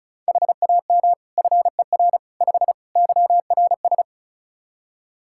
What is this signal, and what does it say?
Morse code, heard as a keyed tone, about 34 wpm, 700 Hz. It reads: HAM FER 5 YRS
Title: K1USN SST QSO Practice 35wpm